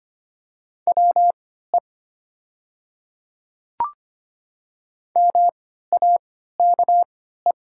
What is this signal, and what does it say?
Morse code, heard as a keyed tone, about 25 wpm, spaced as for 14 wpm, 700 Hz. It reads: WE MAKE
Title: 3x Character Spacing - Top 100 Words - 25wpm